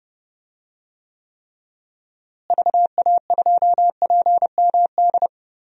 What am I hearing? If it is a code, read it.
Morse code, heard as a keyed tone, about 32 wpm, 700 Hz. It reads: VA2PMD
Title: DXCC-100 Call Signs - 6 Characters - Repeated 3x - 30wpm